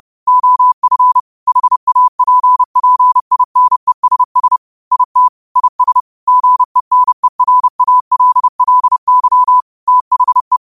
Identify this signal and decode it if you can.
Morse code, heard as a keyed tone, about 30 wpm, 1000 Hz. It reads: OF HAPPINESS IT IS GENERALLY THE
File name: qotdfiles / 30wpm